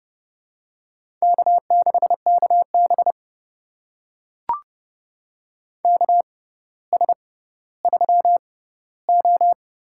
Morse code, heard as a keyed tone, about 30 wpm, spaced as for 15 wpm, 700 Hz. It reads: K6KB KS3O
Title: USA Call Signs - 4 Characters - 6x Character Spacing - 30wpm